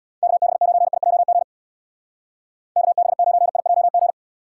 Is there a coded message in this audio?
Yes: Morse code from 0.2 to 4.1 s, heard as a keyed tone, about 65 wpm, 700 Hz. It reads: KB9IJG KB9IJG